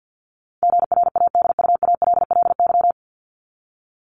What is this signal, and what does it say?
Morse code, heard as a keyed tone, about 50 wpm, 700 Hz. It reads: GRADUALLY
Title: LY Suffix - Top 100 Words with Base Words 50wpm